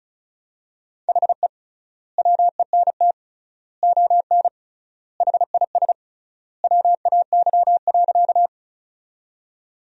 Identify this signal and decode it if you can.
Morse code, heard as a keyed tone, about 36 wpm, 700 Hz. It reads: HE WENT ON HIS WAY.
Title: 3x Word-Spacing - Sentences from Top 200 Words - 35wpm